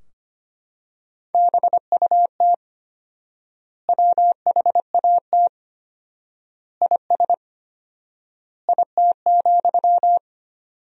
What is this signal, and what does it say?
Morse code, heard as a keyed tone, about 25 wpm, 700 Hz. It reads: BUT WHAT IS IT,
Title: 4x Word-Spacing - Sentences from Top 300 Words - 25wpm